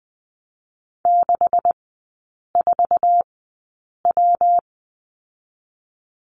Morse code, heard as a keyed tone, about 20 wpm, 700 Hz. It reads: 6 4 W
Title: 127 Lesson 21e - Number 6 - Three Characters 20wpm